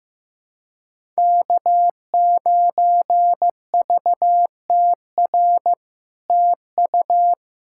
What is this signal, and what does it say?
Morse code, heard as a keyed tone, about 16 wpm, 700 Hz. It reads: K9VTR TU
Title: SOTA QSO Practice 15wpm